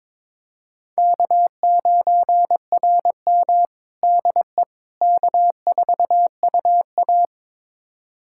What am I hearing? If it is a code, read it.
Morse code, heard as a keyed tone, about 22 wpm, 700 Hz. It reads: K9RM DE K4UA
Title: USA Callsign DE Callsign - 4 Characters - 22wpm